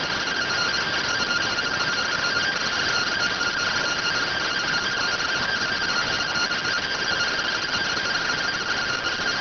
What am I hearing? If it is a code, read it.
Two-tone teleprinter signal, about 170 Hz shift.